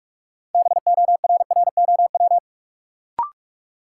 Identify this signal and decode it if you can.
Morse code, heard as a keyed tone, about 42 wpm, 700 Hz. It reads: BORROW
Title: Sentences from Top 1500 Words - New Word Drill Alt - 45wpm